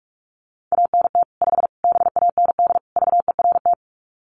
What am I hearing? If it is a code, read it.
Morse code, heard as a keyed tone, about 44 wpm, 700 Hz. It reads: ANT 5 BAND VERT
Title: Common QSO Phrases 45wpm